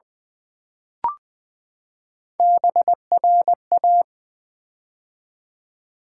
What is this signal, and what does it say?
Morse code, heard as a keyed tone, about 20 wpm, 700 Hz. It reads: BRA
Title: Build A Word - 3 Letters At A Time - 6-Letter Words - 20wpm